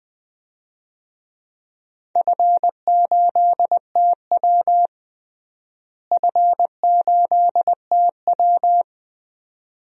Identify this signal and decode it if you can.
Morse code, heard as a keyed tone, about 21 wpm, 700 Hz. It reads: F8TW F8TW
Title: DXCC-100 Call Signs - 4 Characters - Repeated 2x - 20wpm